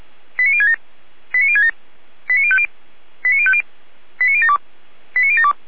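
Five-tone selective call (ZVEI), heard as quick runs of five tones, tones of about 70 ms. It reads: 89067 89067 89050 89050 89072 89072
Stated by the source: ZVEI five-tone selective call received with an SDR.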